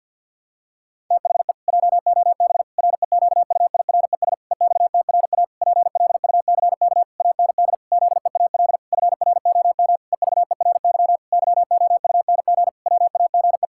Morse code, heard as a keyed tone, about 50 wpm, 700 Hz. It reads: THE JOB REQUIRES EXTRA PLUCK AND ZEAL FROM EVERY YOUNG WAGE
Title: Pangram Challenge v2 - 50wpm